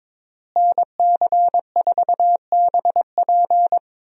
Morse code, heard as a keyed tone, about 22 wpm, 700 Hz. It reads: NC4BP
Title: USA Call Signs - 5 Characters - Repeated 2x - 22wpm